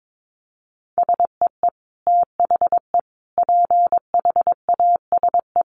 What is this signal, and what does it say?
Morse code, heard as a keyed tone, about 22 wpm, 700 Hz. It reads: SEE THE PHASE
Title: Sentences from Top 1200 Words 22wpm